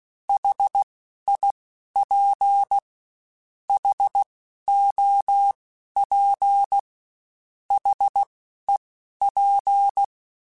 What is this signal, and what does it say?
Morse code, heard as a keyed tone, about 16 wpm, spaced as for 12 wpm, 800 Hz. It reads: HIP HOP HEP
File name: H Prefixes / H mp3s